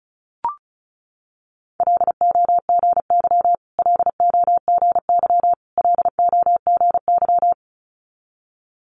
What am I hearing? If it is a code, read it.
Morse code, heard as a keyed tone, about 36 wpm, 700 Hz. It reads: LOGY LOGY LOGY
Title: Multi-letter Phonemes Repeated 3x 35wpm